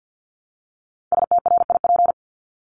Morse code, heard as a keyed tone, about 50 wpm, 700 Hz. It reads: STRIP
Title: Top 1501-1750 Words 50wpm